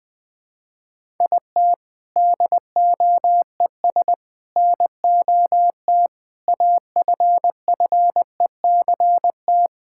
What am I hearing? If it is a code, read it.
Morse code, heard as a keyed tone, about 20 wpm, 700 Hz. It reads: IT DOES NOT AFFECT
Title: Sentences from Top 1000 Words - Review Part 6 - 20wpm